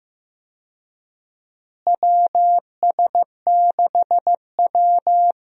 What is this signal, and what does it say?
Morse code, heard as a keyed tone, about 16 wpm, 700 Hz. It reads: WS6W
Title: USA Call Signs - 4 Characters - Repeated 3x - 15wpm